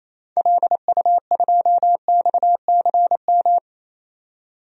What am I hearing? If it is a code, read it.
Morse code, heard as a keyed tone, about 28 wpm, 700 Hz. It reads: LU2XCM
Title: DXCC-100 Call Signs - 6 Characters - 28wpm